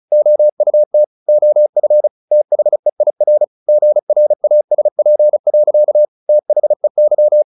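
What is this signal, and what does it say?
Morse code, heard as a keyed tone, about 34 wpm, 600 Hz. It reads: OUT OF THEIR GRASP. THEY
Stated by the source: Morse practice at 35 wpm from CW med Gnister, sending Ken Follett's novel Never in English.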